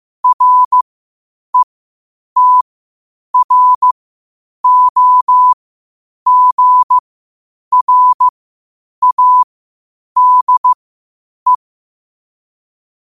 Morse code, heard as a keyed tone, about 15 wpm, spaced as for 8 wpm, 1000 Hz. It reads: RETROGRADE
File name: qotdfiles / 8wpm